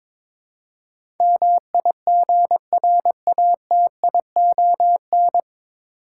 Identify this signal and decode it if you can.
Morse code, heard as a keyed tone, about 21 wpm, 700 Hz. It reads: MIGRATION
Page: Morse Code Ninja practice 22wpm